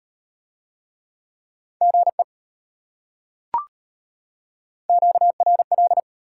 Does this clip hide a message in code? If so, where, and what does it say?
Yes: Morse code from 1.8 to 6.0 s, heard as a keyed tone, about 33 wpm, 700 Hz. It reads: GE QRL